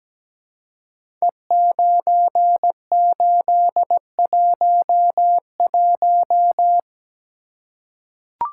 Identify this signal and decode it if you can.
Morse code, heard as a keyed tone, about 17 wpm, 700 Hz. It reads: E9811
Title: Code Groups - 5 Letter:Numbers - 17wpm